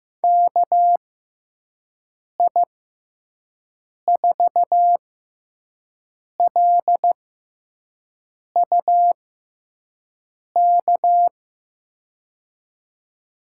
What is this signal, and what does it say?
Morse code, heard as a keyed tone, about 15 wpm, spaced as for 5 wpm, 700 Hz. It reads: KI4LUK
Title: USA Call Signs - 6 Characters - 6x Character Spacing - 15wpm